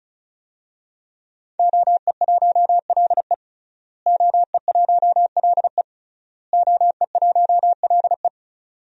Morse code, heard as a keyed tone, about 36 wpm, 700 Hz. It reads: OE1LE OE1LE OE1LE
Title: DXCC-100 Call Signs - 5 Characters - Repeated 3x - 35wpm